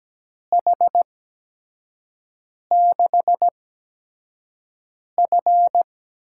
Morse code, heard as a keyed tone, about 17 wpm, spaced as for 5 wpm, 700 Hz. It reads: H6F
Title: USA Call Signs - 4 Characters - 8x Character Spacing - 17wpm